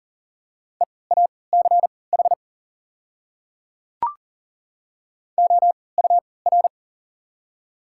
Morse code, heard as a keyed tone, about 40 wpm, spaced as for 23 wpm, 690 Hz. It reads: EACH OUR
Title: 3x Character Spacing - Top 200 Words - 40wpm